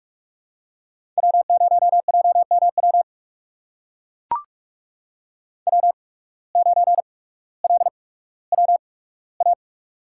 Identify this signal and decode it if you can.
Morse code, heard as a keyed tone, about 45 wpm, spaced as for 19 wpm, 700 Hz. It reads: W0JMW W9LWA
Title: USA Call Signs - 8x Character Spacing - 45wpm